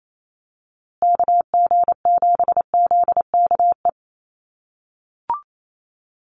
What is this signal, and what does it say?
Morse code, heard as a keyed tone, about 28 wpm, 700 Hz. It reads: KG7ZKE
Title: USA Call Signs Repeated 3x 28wpm